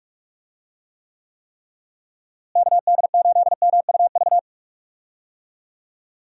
Morse code, heard as a keyed tone, about 44 wpm, 700 Hz. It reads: KD8MUV
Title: USA Call Signs - 6 Characters - Repeated 3x - 45wpm